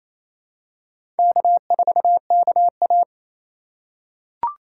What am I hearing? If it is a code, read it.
Morse code, heard as a keyed tone, about 28 wpm, 700 Hz. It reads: K4KA
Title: USA Call Signs - 2x Character Spacing - 28wpm